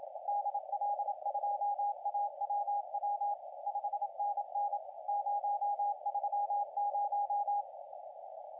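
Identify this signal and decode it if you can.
Morse code, heard as a keyed tone, about 27 wpm, 820 Hz, with radio noise. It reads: DL2AWW 5NN 030
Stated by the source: CW contest audio recorded at LA4C during the SAC CW contest.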